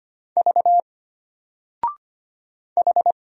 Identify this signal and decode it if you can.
Morse code, heard as a keyed tone, about 25 wpm, 700 Hz. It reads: V H
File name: Single-Letter-Number-25wpm